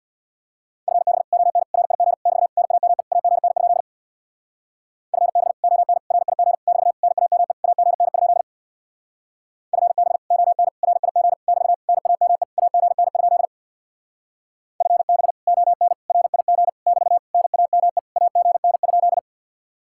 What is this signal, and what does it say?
Morse code, heard as a keyed tone, about 64 wpm, 700 Hz. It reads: FB ON RIG <BT> NAME AGN? FB ON RIG <BT> NAME AGN? FB ON RIG <BT> NAME AGN? FB ON RIG <BT> NAME AGN?